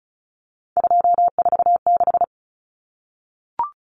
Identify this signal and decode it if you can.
Morse code, heard as a keyed tone, about 34 wpm, 700 Hz. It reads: 246
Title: Common Cut Numbers - 3-Digit 35wpm